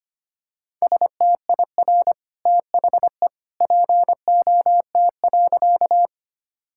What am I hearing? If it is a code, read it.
Morse code, heard as a keyed tone, about 25 wpm, 700 Hz. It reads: STIR THE POT.
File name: Easy Sentences with Hints - 25wpm